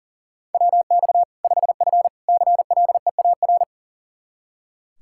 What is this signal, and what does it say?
Morse code, heard as a keyed tone, about 40 wpm, 700 Hz. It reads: WX 5F CLEAR
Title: Practice QSO - Exchange 2 - Info - 40wpm